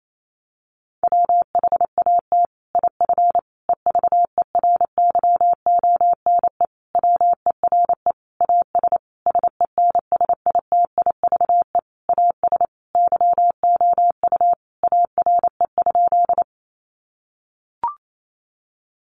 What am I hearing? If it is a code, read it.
Morse code, heard as a keyed tone, about 28 wpm, 700 Hz. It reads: WHAT IF EVERYONE WERE AS SENSITIVE AS YOU ARE?
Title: Sentences from Top 1400 Words 28wpm